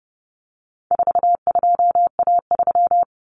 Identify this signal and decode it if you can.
Morse code, heard as a keyed tone, about 30 wpm, 700 Hz. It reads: 42A3
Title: Common Cut Numbers - 4-Digit 30wpm